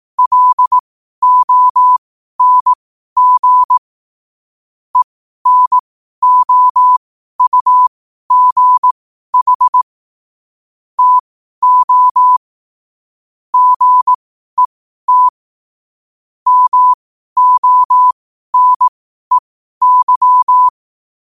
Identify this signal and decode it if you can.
Morse code, heard as a keyed tone, about 18 wpm, spaced as for 12 wpm, 1000 Hz. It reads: LONG ENOUGH TO GET MONEY
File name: qotdfiles / 12wpm